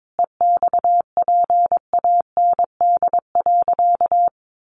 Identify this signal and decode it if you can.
Morse code, heard as a keyed tone, about 22 wpm, 700 Hz. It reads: EXPAND.
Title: Sentences from Top 1400 Words 22wpm